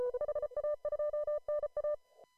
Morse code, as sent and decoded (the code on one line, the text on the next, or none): .... .- ..--- -. .-
HA2NA